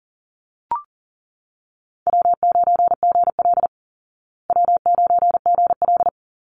Morse code, heard as a keyed tone, about 40 wpm, 700 Hz. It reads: W9GL W9GL
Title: Call Signs Repeated 3x - 40wpm